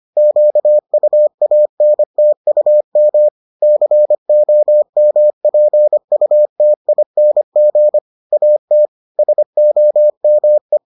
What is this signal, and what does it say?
Morse code, heard as a keyed tone, about 25 wpm, 600 Hz. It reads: QUANTUM COMPUTING AT SOME